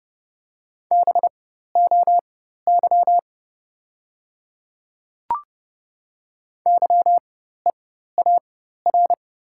Morse code, heard as a keyed tone, about 30 wpm, spaced as for 14 wpm, 700 Hz. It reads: BOY YEAR
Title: 4x Character Spacing - Top 200 Words - 30wpm